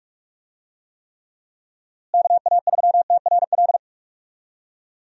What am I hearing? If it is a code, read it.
Morse code, heard as a keyed tone, about 46 wpm, 700 Hz. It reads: KA3TRL
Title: USA Call Signs - 6 Characters - Repeated 3x - 45wpm